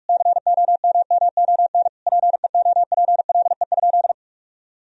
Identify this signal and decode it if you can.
Morse code, heard as a keyed tone, about 46 wpm, 700 Hz. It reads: COMMON PEOPLE?
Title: Sentences from Top 1000 Words - Review Part 3 - 45wpm